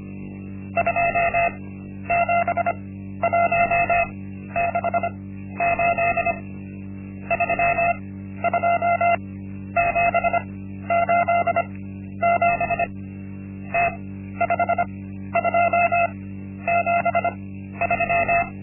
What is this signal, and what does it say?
Morse code, heard as a keyed tone, about 25 wpm, spaced as for 12 wpm, 690 Hz. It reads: 27168 32787 T5273
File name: Linksmakalnis-1